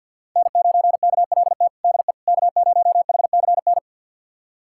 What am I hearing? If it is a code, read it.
Morse code, heard as a keyed tone, about 50 wpm, 700 Hz. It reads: N9KLT DE K0HCN
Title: USA Callsign DE Callsign - 5 Characters - 50wpm